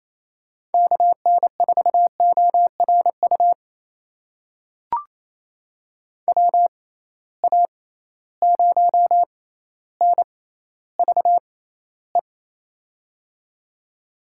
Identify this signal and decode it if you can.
Morse code, heard as a keyed tone, about 27 wpm, spaced as for 14 wpm, 700 Hz. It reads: KN4ORU WA0NVE
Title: USA Call Signs - 6 Characters - 6x Character Spacing - 28wpm